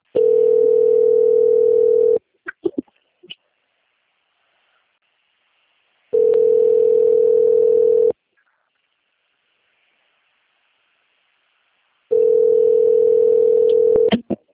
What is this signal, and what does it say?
Ringback tone (North America), 440 + 480 Hz.